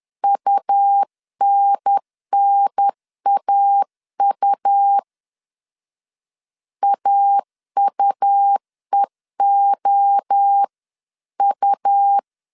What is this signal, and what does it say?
Morse code, heard as a keyed tone, about 10 wpm, 800 Hz. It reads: UNNAU AUEOU
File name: 10wpm